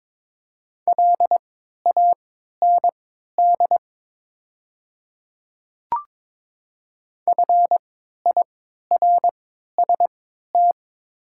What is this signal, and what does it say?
Morse code, heard as a keyed tone, about 22 wpm, spaced as for 12 wpm, 700 Hz. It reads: LAND FIRST
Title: 3x Character Spacing - Top 200 Words - 22wpm